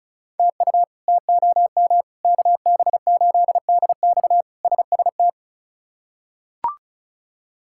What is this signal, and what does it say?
Morse code, heard as a keyed tone, about 36 wpm, 700 Hz. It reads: TU TOM KB8DX SST
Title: K1USN SST QSO Practice - 35wpm